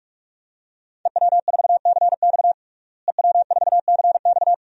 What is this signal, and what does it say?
Morse code, heard as a keyed tone, about 46 wpm, 700 Hz. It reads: EW4CX EW4CX
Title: DXCC-100 Call Signs - 5 Characters - Repeated 2x - 45wpm